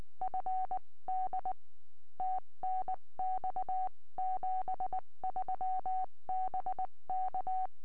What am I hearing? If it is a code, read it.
Morse code, heard as a keyed tone, about 19 wpm, 750 Hz. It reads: FD TNX 73 BK
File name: tropo / g3ltf 70cm